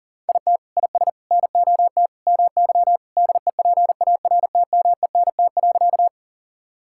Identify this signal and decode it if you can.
Morse code, heard as a keyed tone, about 40 wpm, 700 Hz. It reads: IT IS NOT MY DEPARTMENT.
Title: Sentences from Top 1000 Words - Review Part 5 - 40wpm